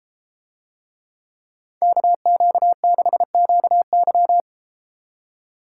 Morse code, heard as a keyed tone, about 33 wpm, 700 Hz. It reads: KQ6QY